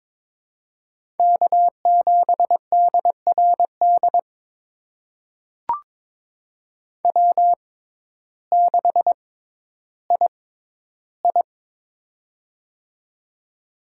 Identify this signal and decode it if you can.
Morse code, heard as a keyed tone, about 22 wpm, spaced as for 12 wpm, 700 Hz. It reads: K7DRD W6II
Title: USA Call Signs - 6x Character Spacing - 22wpm